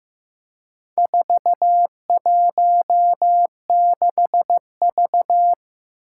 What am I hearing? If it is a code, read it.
Morse code, heard as a keyed tone, about 15 wpm, 700 Hz. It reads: 416V
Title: Four Challenging Letter-Number Combinations 15wpm